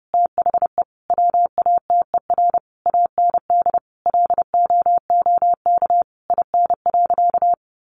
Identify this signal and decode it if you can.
Morse code, heard as a keyed tone, about 30 wpm, 700 Hz. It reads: THE WATER AND LOOK IN.
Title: Sentences from Top 1300 Words 30wpm